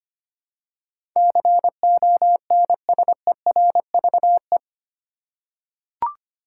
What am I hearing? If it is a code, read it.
Morse code, heard as a keyed tone, about 25 wpm, 700 Hz. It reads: CONSERVE
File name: Eight Letter Words - 25wpm